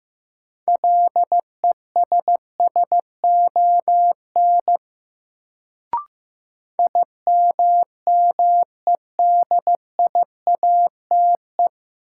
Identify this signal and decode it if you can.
Morse code, heard as a keyed tone, about 15 wpm, 700 Hz. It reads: LESSON IMMEDIATE
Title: Sentences from Top 1200 Words - New Word Drill Alt - 15wpm